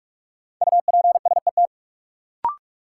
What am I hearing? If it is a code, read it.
Morse code, heard as a keyed tone, about 46 wpm, 700 Hz. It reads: UPSET
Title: Sentences from Top 800 Words - New Word Drill - 45wpm